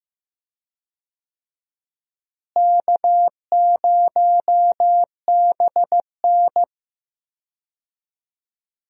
Morse code, heard as a keyed tone, about 15 wpm, 700 Hz. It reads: K0BN
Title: USA Call Signs - 4 Characters - Repeated 3x - 15wpm